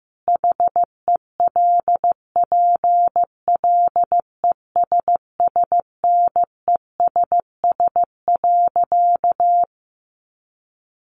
Morse code, heard as a keyed tone, about 15 wpm, 700 Hz. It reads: HELPLESSNESS.
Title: A Sound Of Thunder 15wpm